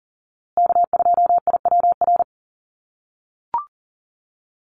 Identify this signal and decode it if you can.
Morse code, heard as a keyed tone, about 40 wpm, 700 Hz. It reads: K2IWR
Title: USA Call Signs Repeated 3x 40wpm